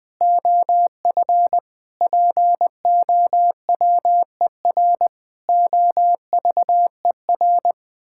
Morse code, heard as a keyed tone, about 20 wpm, 700 Hz. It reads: OF POWER OVER
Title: Sentences from Top 1000 Words - Review Part 5 - 20wpm